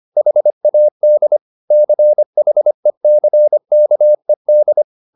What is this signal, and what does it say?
Morse code, heard as a keyed tone, about 25 wpm, 600 Hz. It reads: HAD CHECKED